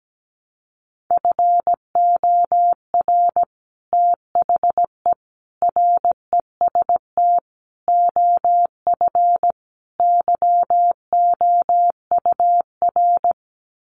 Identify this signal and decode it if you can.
Morse code, heard as a keyed tone, about 17 wpm, 700 Hz. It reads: FOR THE REST OF YOUR
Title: Sets of 5 Words Encore 17wpm